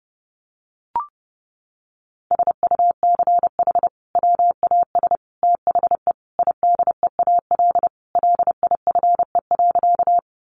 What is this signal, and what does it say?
Morse code, heard as a keyed tone, about 30 wpm, 700 Hz. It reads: SUCH WAS THE IDEAL LIFE.
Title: Sentences from Top 1400 Words 30wpm